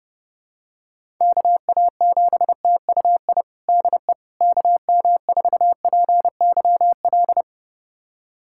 Morse code, heard as a keyed tone, about 28 wpm, 700 Hz. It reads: KA7TUI DE KM4PYL
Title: USA Callsign DE Callsign 30wpm